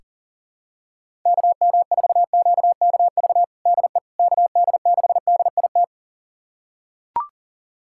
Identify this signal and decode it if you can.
Morse code, heard as a keyed tone, about 37 wpm, 700 Hz. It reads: KM4QKV DE KD6DIT
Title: USA Callsign DE Callsign - 6 Characters - 40wpm